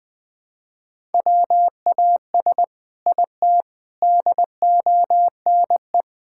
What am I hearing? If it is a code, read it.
Morse code, heard as a keyed tone, about 20 wpm, 700 Hz. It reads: WAS IT DONE
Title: Sentences from Top 1000 Words - Review Part 6 - 20wpm